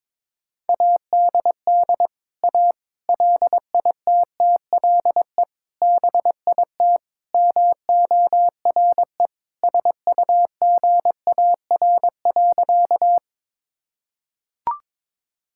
Morse code, heard as a keyed tone, about 22 wpm, 700 Hz. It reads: ADD A LITTLE BIT MORE SUGAR.
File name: practice / 22wpm / Sentences with 4-Word Phrases - 22wpm